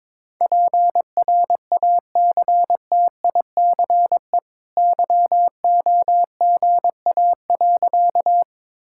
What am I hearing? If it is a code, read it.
Morse code, heard as a keyed tone, about 22 wpm, 690 Hz. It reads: PRACTICE YOGA.
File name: practice / 22wpm / Easy Sentences with Hints - 22wpm